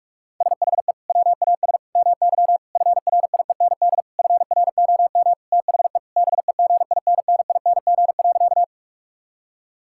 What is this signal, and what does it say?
Morse code, heard as a keyed tone, about 46 wpm, 700 Hz. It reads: SHE WAS MY FRIEND FROM THE BEGINNING.
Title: Sentences from Top 1000 Words - Review Part 5 - 45wpm